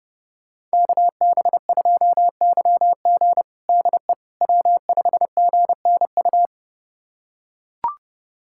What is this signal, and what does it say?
Morse code, heard as a keyed tone, about 32 wpm, 700 Hz. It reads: KB2YG DE W5GNU
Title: USA Callsign DE Callsign - 5 Characters - 30wpm